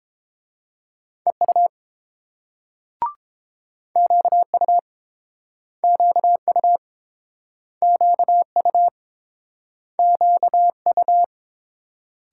Morse code, heard as a keyed tone, about 29 wpm, 700 Hz. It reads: EU QU QU QU QU